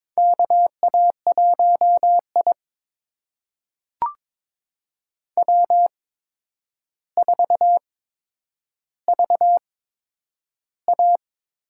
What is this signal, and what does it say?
Morse code, heard as a keyed tone, about 22 wpm, spaced as for 9 wpm, 700 Hz. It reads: KA1I W4VA
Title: USA Call Signs - 4 Characters - 8x Character Spacing - 22wpm